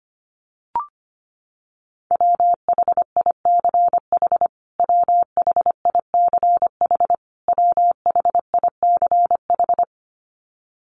Morse code, heard as a keyed tone, about 25 wpm, 700 Hz. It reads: WHICH WHICH WHICH
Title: Top 500 Words Repeated 25wpm